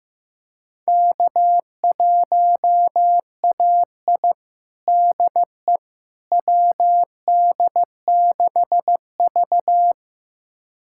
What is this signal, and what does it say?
Morse code, heard as a keyed tone, about 15 wpm, 700 Hz. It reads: K1AI DE WD6V
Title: USA Callsign DE Callsign - 4 Characters - 15wpm